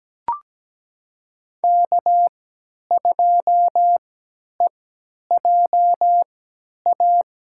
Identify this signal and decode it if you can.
Morse code, heard as a keyed tone, about 17 wpm, spaced as for 10 wpm, 700 Hz. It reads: K2EJA
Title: USA Call Signs - 5 Characters - 3x Character Spacing - 17wpm